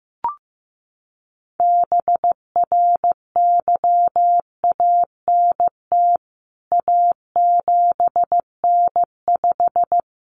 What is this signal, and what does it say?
Morse code, heard as a keyed tone, about 15 wpm, 700 Hz. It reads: BRYANT A7N5
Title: CWT Exchange 15wpm